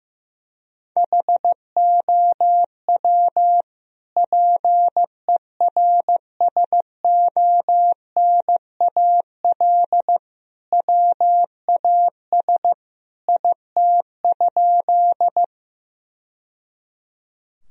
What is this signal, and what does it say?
Morse code, heard as a keyed tone, about 15 wpm, 700 Hz. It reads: HOW PERSONAL WAS IT?
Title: Sentences from Top 1000 Words - Review Part 4 - 15wpm